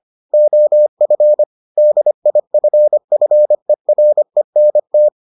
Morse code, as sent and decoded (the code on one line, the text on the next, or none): --- ..-. / -.. .. ..-. ..-. .-. -. -
OF DIFFERENT